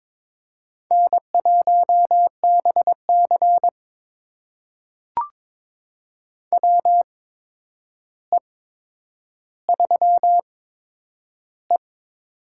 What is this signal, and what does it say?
Morse code, heard as a keyed tone, about 22 wpm, spaced as for 9 wpm, 700 Hz. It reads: N1BC WE3E
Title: USA Call Signs - 4 Characters - 8x Character Spacing - 22wpm